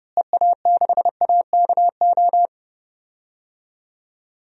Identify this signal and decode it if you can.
Morse code, heard as a keyed tone, about 30 wpm, 700 Hz. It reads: EA6AKO
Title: DXCC-100 Call Signs - 6 Characters - Repeated 3x - 30wpm